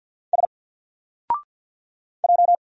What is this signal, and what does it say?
Morse code, heard as a keyed tone, about 49 wpm, 700 Hz. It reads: S J
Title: Single Character (Letter:Number:Punctuation:Prosign) 50wpm